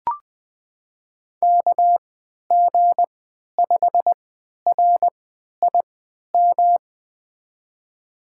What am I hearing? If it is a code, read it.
Morse code, heard as a keyed tone, about 20 wpm, spaced as for 11 wpm, 700 Hz. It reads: KG5RIM
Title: USA Call Signs - 6 Characters - 3x Character Spacing - 20wpm